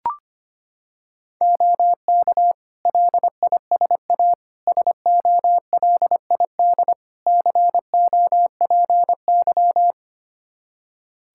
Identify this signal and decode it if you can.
Morse code, heard as a keyed tone, about 25 wpm, 700 Hz. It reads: OK LISA SOLID COPY
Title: Practice QSO - Exchange 3 - More Info - 25wpm